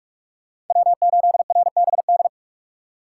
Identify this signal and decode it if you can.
Morse code, heard as a keyed tone, about 46 wpm, 690 Hz. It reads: W8RBD
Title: USA Call Signs - 3x Character Spacing - 45wpm